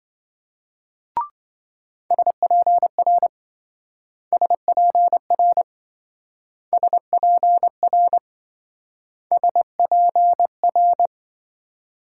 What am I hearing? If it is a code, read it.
Morse code, heard as a keyed tone, about 26 wpm, 700 Hz. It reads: SPR SPR SPR SPR